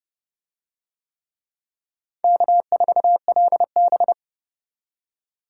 Morse code, heard as a keyed tone, about 29 wpm, 700 Hz. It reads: K4LB